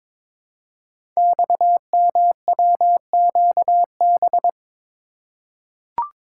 Morse code, heard as a keyed tone, about 22 wpm, 700 Hz. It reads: XMWQB
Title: Code Groups - 5 Letter:Numbers - 22wpm